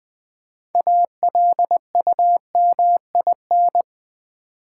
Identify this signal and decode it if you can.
Morse code, heard as a keyed tone, about 20 wpm, 700 Hz. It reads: ALUMIN
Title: Build A Word - 3 Letters At A Time - 8-Letter Words - 20wpm